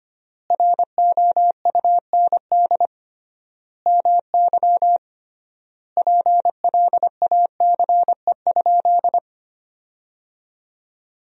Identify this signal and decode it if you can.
Morse code, heard as a keyed tone, about 25 wpm, 700 Hz. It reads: ROUND MY PLACE?
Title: 3x Word-Spacing - Sentences from Top 200 Words - 25wpm